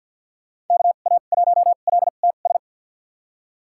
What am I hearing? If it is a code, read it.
Morse code, heard as a keyed tone, about 51 wpm, spaced as for 36 wpm, 700 Hz. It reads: KA1LTS
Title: USA Call Signs - 2x Character Spacing - 50wpm